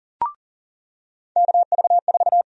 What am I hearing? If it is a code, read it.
Morse code, heard as a keyed tone, about 40 wpm, 700 Hz. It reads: KV4
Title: USA Call Sign Prefixes - 40wpm